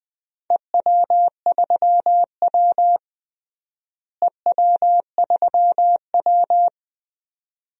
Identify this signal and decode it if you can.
Morse code, heard as a keyed tone, about 20 wpm, 700 Hz. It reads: EW3W EW3W
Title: DXCC-100 Call Signs - 4 Characters - Repeated 2x - 20wpm